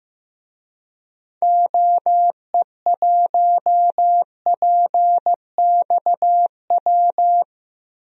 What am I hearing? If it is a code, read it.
Morse code, heard as a keyed tone, about 16 wpm, 700 Hz. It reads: OE1PXW
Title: DXCC-100 Call Signs - 6 Characters - Repeated 2x - 15wpm